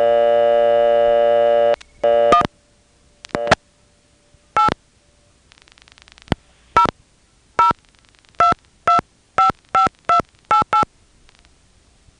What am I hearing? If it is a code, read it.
DTMF, heard as telephone touch tones, tones of about 120 ms. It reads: 1 8 002255288
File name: phonetrips / Nantes1HQ